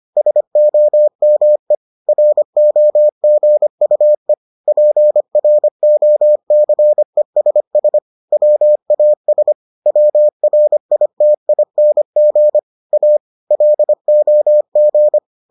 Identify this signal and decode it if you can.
Morse code, heard as a keyed tone, about 25 wpm, 600 Hz. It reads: SOME ROGUE PROCESS WAS WRITING A LOG